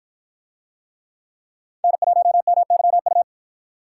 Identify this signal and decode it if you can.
Morse code, heard as a keyed tone, about 53 wpm, 700 Hz. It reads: N1MYU